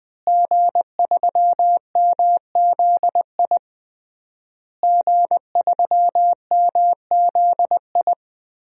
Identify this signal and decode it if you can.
Morse code, heard as a keyed tone, about 20 wpm, 700 Hz. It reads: G3MZI G3MZI
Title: DXCC-100 Call Signs - 5 Characters - Repeated 3x - 20wpm